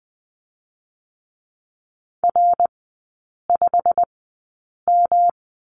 Morse code, heard as a keyed tone, about 20 wpm, spaced as for 8 wpm, 700 Hz. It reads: R5M
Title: 152 Lesson 25e - Letter p - Three Characters 20wpm